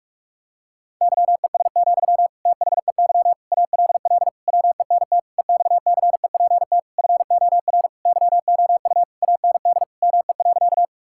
Morse code, heard as a keyed tone, about 46 wpm, 700 Hz. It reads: YES, THEY ALL WENT EXCEPT FOR YOU AND ME.
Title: Sentences from Top 1600 Words 45wpm